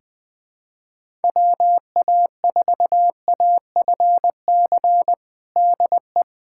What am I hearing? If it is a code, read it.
Morse code, heard as a keyed tone, about 21 wpm, 700 Hz. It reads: WA4AFC DE
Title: USA Callsign DE Callsign - 6 Characters - 20wpm